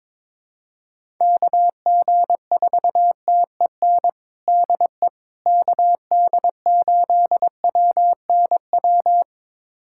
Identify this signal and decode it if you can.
Morse code, heard as a keyed tone, about 22 wpm, 700 Hz. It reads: KG4TEN DE KD8WNW
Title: USA Callsign DE Callsign - 6 Characters - 22wpm